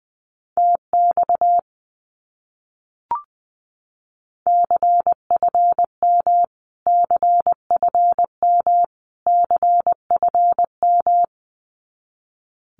Morse code, heard as a keyed tone, about 20 wpm, 700 Hz. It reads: TX CFM CFM CFM
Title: Abbreviations Repeated 3x - 20wpm